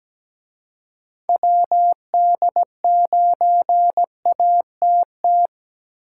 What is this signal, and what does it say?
Morse code, heard as a keyed tone, about 20 wpm, 700 Hz. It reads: WD9ATT
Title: Speed Racing - USA 2x3 Call Signs 17wpm